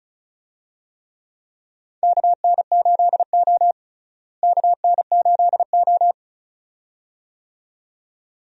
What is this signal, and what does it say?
Morse code, heard as a keyed tone, about 36 wpm, 700 Hz. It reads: KN8O KN8O
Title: USA Call Signs - 4 Characters - Repeated 2x - 35wpm